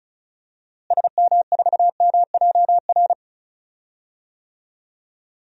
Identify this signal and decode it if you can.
Morse code, heard as a keyed tone, about 36 wpm, 700 Hz. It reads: SM4MJR
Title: DXCC-100 Call Signs - 6 Characters - Repeated 2x - 35wpm